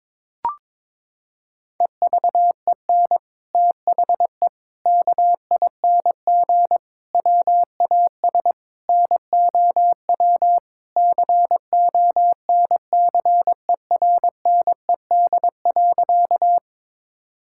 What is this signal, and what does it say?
Morse code, heard as a keyed tone, about 21 wpm, 700 Hz. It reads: EVEN THE KING WAS NOW CONCERNED.
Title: Sentences from Top 1000 Words - Review Part 6 - 22wpm